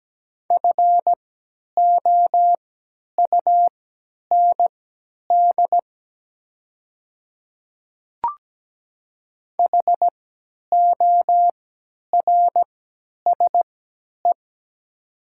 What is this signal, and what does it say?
Morse code, heard as a keyed tone, about 17 wpm, spaced as for 10 wpm, 700 Hz. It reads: FOUND HORSE
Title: 3x Character Spacing - Top 500 Words - 17wpm